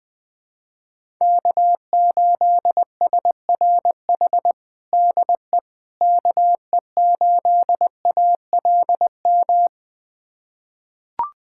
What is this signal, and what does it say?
Morse code, heard as a keyed tone, about 21 wpm, 700 Hz. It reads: K8SRH DE KE8ALM
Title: USA Callsign DE Callsign 20wpm